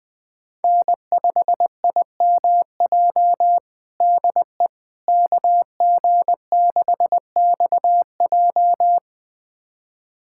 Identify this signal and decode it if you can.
Morse code, heard as a keyed tone, about 21 wpm, 700 Hz. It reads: N5IMJ DE KG6XJ
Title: USA Callsign DE Callsign - 5 Characters - 20wpm